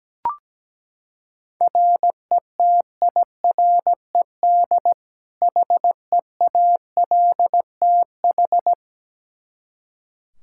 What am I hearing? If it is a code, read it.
Morse code, heard as a keyed tone, about 17 wpm, 700 Hz. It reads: RETIRED HEALTH
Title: Practice QSO - Exchange 3 - More Info - 17wpm